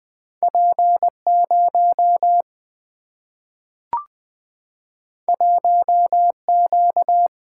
Morse code, heard as a keyed tone, about 20 wpm, 700 Hz. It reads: P0 1Q
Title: Single-Letter Single-Number 20wpm